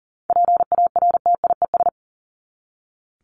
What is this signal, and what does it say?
Morse code, heard as a keyed tone, about 40 wpm, 700 Hz. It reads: PARTIES
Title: ES:S Suffix - Top 100 Words with Base Words 40wpm